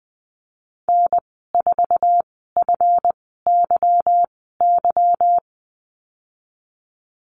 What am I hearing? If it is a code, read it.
Morse code, heard as a keyed tone, about 20 wpm, spaced as for 15 wpm, 700 Hz. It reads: N4FYY
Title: 193 Lesson 31g - Letter v - Call Signs 20wpm